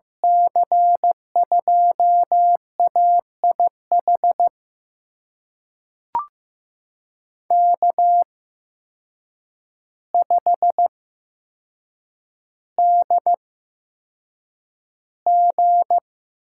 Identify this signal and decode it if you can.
Morse code, heard as a keyed tone, about 15 wpm, spaced as for 7 wpm, 700 Hz. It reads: C2AIH K5DG
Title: USA Call Signs - 8x Character Spacing - 15wpm